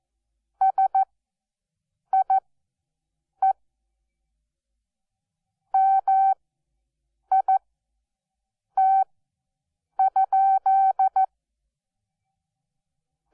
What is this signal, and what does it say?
Morse code, heard as a keyed tone, about 14 wpm, spaced as for 6 wpm, 770 Hz. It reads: SIE MIT?
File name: Onlinelehrgang / morsen / DJ4UF Morsen12